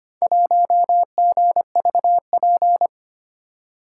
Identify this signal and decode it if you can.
Morse code, heard as a keyed tone, about 25 wpm, 700 Hz. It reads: 1GVP
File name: practice / 25wpm / Four Letter-Number Combinations - 25wpm